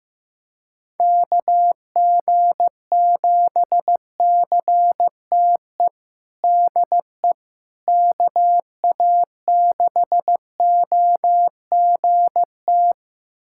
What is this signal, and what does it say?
Morse code, heard as a keyed tone, about 15 wpm, 700 Hz. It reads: KG7CTE DE KA6OGT
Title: USA Callsign DE Callsign - 6 Characters - 15wpm